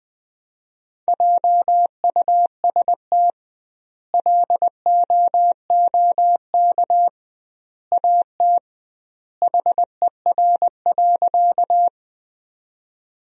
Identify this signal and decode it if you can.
Morse code, heard as a keyed tone, about 20 wpm, 700 Hz. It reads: JUST LOOK AT HER.
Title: 2x Word-Spacing - Sentences from Top 200 Words - 20wpm